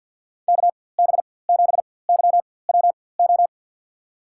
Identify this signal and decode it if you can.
Morse code, heard as a keyed tone, about 52 wpm, spaced as for 23 wpm, 700 Hz. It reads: KB7YWO
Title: USA Call Signs - 6 Characters - 4x Character Spacing - 50wpm